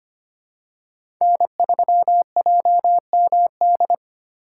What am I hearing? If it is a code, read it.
Morse code, heard as a keyed tone, about 25 wpm, 700 Hz. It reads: N3JMD
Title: USA Call Signs - 5 Characters - 4x Character Spacing - 25wpm